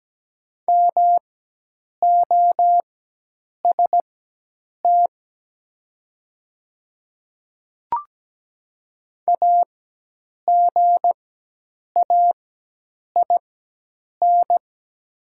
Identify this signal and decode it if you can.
Morse code, heard as a keyed tone, about 17 wpm, spaced as for 8 wpm, 700 Hz. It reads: MOST AGAIN
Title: 4x Character Spacing - Top 300 Words - 17wpm